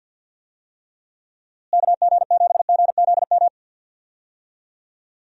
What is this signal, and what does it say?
Morse code, heard as a keyed tone, about 48 wpm, 700 Hz. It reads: KG7GZM
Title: USA Call Signs - 6 Characters - Repeated 2x - 50wpm